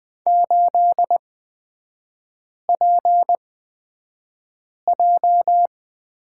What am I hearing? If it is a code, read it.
Morse code, heard as a keyed tone, about 20 wpm, spaced as for 5 wpm, 700 Hz. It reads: 8PJ
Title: 255 Lesson 41e - Period - Three Characters 20wpm